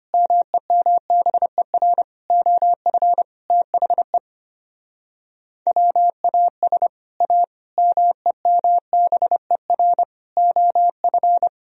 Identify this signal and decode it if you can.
Morse code, heard as a keyed tone, about 32 wpm, 700 Hz. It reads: MEMBER OF THE WAS A MEMBER OF